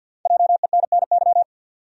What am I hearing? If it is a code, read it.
Morse code, heard as a keyed tone, about 48 wpm, 700 Hz. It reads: JENNY
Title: USA Common Names - Max 5 Letters - 50wpm